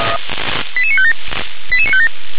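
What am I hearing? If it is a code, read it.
Five-tone selective call (ZVEI), heard as quick runs of five tones, tones of about 70 ms. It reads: 88957 88957
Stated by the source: ZVEI five-tone call received by SDR, from the feuerwehr-orlamuende.org (fire brigade Orlamünde) archive.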